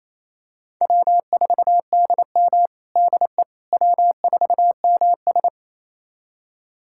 Morse code, heard as a keyed tone, about 28 wpm, 700 Hz. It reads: W4DM DE W4MS
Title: USA Callsign DE Callsign - 4 Characters - 28wpm